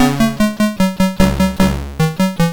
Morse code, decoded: none